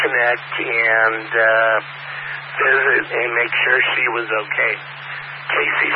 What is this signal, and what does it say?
Emergency Alert System (SAME) tones, heard as harsh buzzing bursts, with speech. It is heard from 0.7 to 4.8 s.